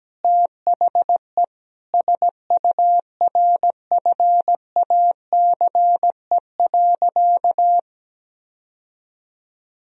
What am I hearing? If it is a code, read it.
Morse code, heard as a keyed tone, about 17 wpm, 700 Hz. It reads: THE SURFACE.
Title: Sentences from Top 1000 Words - Review Part 3 - 17wpm